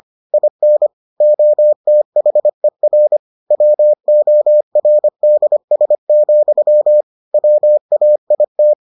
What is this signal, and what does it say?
Morse code, heard as a keyed tone, about 25 wpm, 600 Hz. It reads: IN OTHER WORDS, WAIT